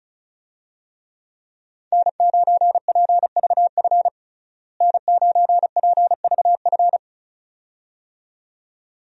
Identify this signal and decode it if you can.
Morse code, heard as a keyed tone, about 36 wpm, 700 Hz. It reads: N9PVF N9PVF
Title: USA Call Signs - 5 Characters - Repeated 2x - 35wpm